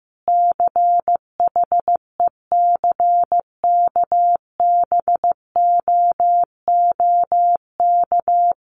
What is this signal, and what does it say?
Morse code, heard as a keyed tone, about 15 wpm, 700 Hz. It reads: CHECKBOOK
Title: A Sound Of Thunder 15wpm